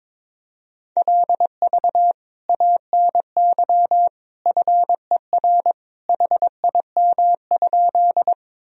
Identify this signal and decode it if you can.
Morse code, heard as a keyed tone, about 22 wpm, 700 Hz. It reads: LV ANY FER HIM?
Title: Sentences from Top 400 Words and Abbreviations 22wpm